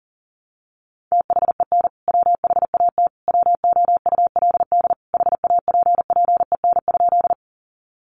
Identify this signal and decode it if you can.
Morse code, heard as a keyed tone, about 40 wpm, 700 Hz. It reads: THEN WHAT WOULD HAPPEN?
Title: Sentences from Top 400 Words 40wpm